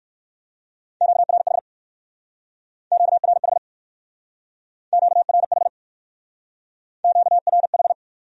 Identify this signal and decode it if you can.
Morse code, heard as a keyed tone, about 56 wpm, 700 Hz. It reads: QRH QRH QRH QRH
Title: Speed-Racing Q-Codes - 45wpm